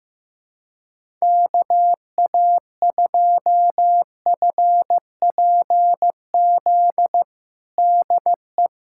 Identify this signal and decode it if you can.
Morse code, heard as a keyed tone, about 15 wpm, 700 Hz. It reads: KA2FPZ DE
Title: USA Callsign DE Callsign - 6 Characters - 15wpm